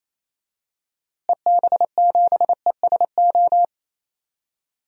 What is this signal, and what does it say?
Morse code, heard as a keyed tone, about 27 wpm, 700 Hz. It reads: EB7ESO
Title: DXCC-100 Call Signs - 6 Characters - Repeated 2x - 28wpm